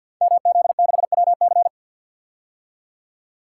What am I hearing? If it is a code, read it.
Morse code, heard as a keyed tone, about 52 wpm, 700 Hz. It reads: MZ6WC
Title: Code Groups - 5 Letter:Numbers - 50wpm